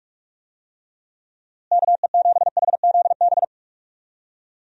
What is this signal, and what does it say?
Morse code, heard as a keyed tone, about 46 wpm, 700 Hz. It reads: KE7HZB
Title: USA Call Signs - 6 Characters - Repeated 2x - 45wpm